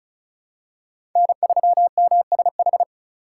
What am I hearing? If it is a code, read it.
Morse code, heard as a keyed tone, about 34 wpm, 700 Hz. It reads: N3MSH
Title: USA Call Signs - 5 Characters - 4x Character Spacing - 35wpm